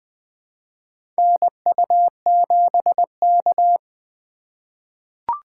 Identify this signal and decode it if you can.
Morse code, heard as a keyed tone, about 20 wpm, 700 Hz. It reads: NU7K
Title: USA Call Signs - 4 Characters - 3x Character Spacing - 20wpm